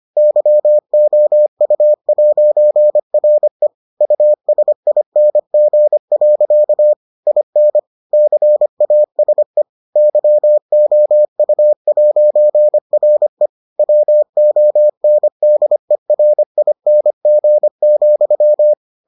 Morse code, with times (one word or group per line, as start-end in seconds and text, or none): 0.2-3.7 YOU'RE
4.0-6.9 USING.
7.3-7.8 IN
8.1-9.6 CASE
10.0-13.5 YOU'RE
13.8-18.7 WONDERING,